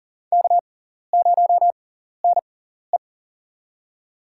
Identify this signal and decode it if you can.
Morse code, heard as a keyed tone, about 40 wpm, spaced as for 14 wpm, 700 Hz. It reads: K0NE
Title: USA Call Signs - 6x Character Spacing - 40wpm